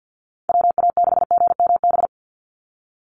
Morse code, heard as a keyed tone, about 50 wpm, 700 Hz. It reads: WA6GMB
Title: USA Call Signs 50wpm